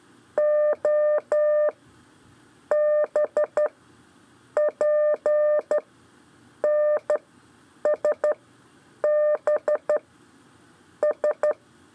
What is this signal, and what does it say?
Morse code, heard as a keyed tone, about 12 wpm, 600 Hz. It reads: O B P N S B S
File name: 4 LEZIONE CW 1